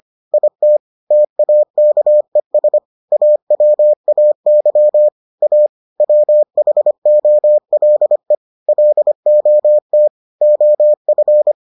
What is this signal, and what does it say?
Morse code, heard as a keyed tone, about 25 wpm, 600 Hz. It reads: IT TAKES AWAY A WHOLE LOT OF